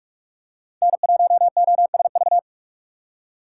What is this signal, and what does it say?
Morse code, heard as a keyed tone, about 44 wpm, 700 Hz. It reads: N1OSV
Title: USA Call Signs - 5 Characters - 3x Character Spacing - 45wpm